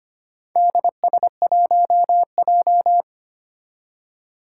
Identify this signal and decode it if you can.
Morse code, heard as a keyed tone, about 25 wpm, 700 Hz. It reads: DS1J
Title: Four Challenging Letter-Number Combinations 25wpm